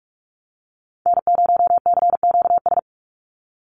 Morse code, heard as a keyed tone, about 44 wpm, 700 Hz. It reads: N0CQS
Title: USA Call Signs 45wpm